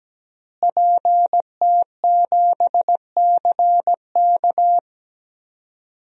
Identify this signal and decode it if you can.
Morse code, heard as a keyed tone, about 17 wpm, 700 Hz. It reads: PT7CK
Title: DXCC-100 Call Signs - 5 Characters - Repeated 2x - 17wpm